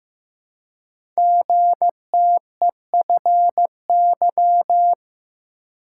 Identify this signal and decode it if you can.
Morse code, heard as a keyed tone, about 15 wpm, 700 Hz. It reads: GTEFY
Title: Code Groups - 5 Letter:Numbers - 15wpm